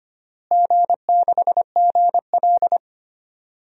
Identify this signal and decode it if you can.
Morse code, heard as a keyed tone, about 25 wpm, 700 Hz. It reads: G6GL
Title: DXCC-100 Call Signs - 4 Characters - Repeated 2x - 25wpm